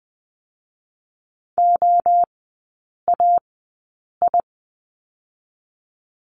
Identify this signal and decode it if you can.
Morse code, heard as a keyed tone, about 20 wpm, 700 Hz. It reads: O A I
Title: 029 Lesson 6e - Letter i - Three Characters 20wpm